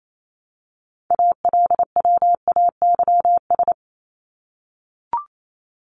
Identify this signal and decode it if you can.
Morse code, heard as a keyed tone, about 28 wpm, 700 Hz. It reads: ALWAYS
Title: Top 300 Words Repeated 28wpm